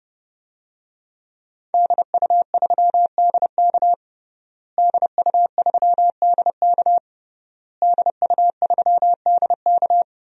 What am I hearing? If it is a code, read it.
Morse code, heard as a keyed tone, about 32 wpm, 700 Hz. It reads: DU3DK DU3DK DU3DK
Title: DXCC-100 Call Signs - 5 Characters - Repeated 3x - 30wpm